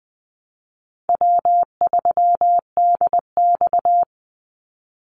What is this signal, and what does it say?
Morse code, heard as a keyed tone, about 20 wpm, 700 Hz. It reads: W3DX
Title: NA SOTA Activator Callsigns 20wpm